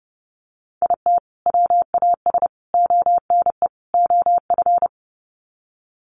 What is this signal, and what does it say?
Morse code, heard as a keyed tone, about 30 wpm, 700 Hz. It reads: IT WAS ONE OF
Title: Sets of 4 Words 30wpm